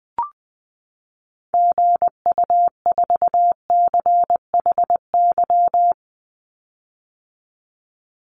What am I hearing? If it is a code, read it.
Morse code, heard as a keyed tone, about 20 wpm, 700 Hz. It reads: GU4CHY
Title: Call Signs - 20wpm